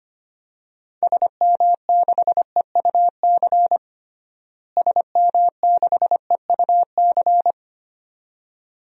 Morse code, heard as a keyed tone, about 25 wpm, 700 Hz. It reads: SM6EUC SM6EUC
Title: DXCC-100 Call Signs - 6 Characters - Repeated 2x - 25wpm